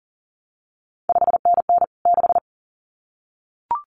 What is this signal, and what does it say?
Morse code, heard as a keyed tone, about 41 wpm, 700 Hz. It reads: 5NN 6
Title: CQ WW DX Contest Exchange Encore - 40wpm